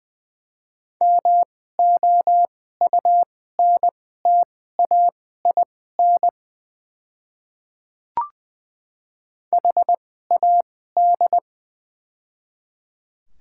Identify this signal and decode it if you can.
Morse code, heard as a keyed tone, about 20 wpm, spaced as for 14 wpm, 700 Hz. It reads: MOUNTAIN HAD
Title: 2x Character Spacing - Top 300 Words - 20wpm